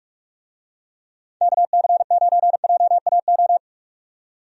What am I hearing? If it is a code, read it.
Morse code, heard as a keyed tone, about 46 wpm, 700 Hz. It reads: KC9JAO
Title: USA Call Signs - 6 Characters - 45wpm